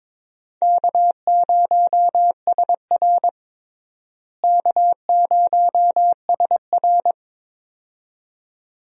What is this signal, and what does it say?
Morse code, heard as a keyed tone, about 22 wpm, 700 Hz. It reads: K0SR K0SR
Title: USA Call Signs - 4 Characters - Repeated 2x - 22wpm